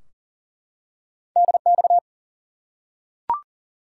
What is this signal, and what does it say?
Morse code, heard as a keyed tone, about 40 wpm, 700 Hz. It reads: DX
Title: Top 90 QSO Elements 40wpm